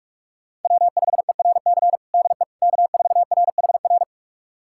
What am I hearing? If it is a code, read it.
Morse code, heard as a keyed tone, about 41 wpm, 700 Hz. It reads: W5ERC DE K4RHR
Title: USA Callsign DE Callsign 45wpm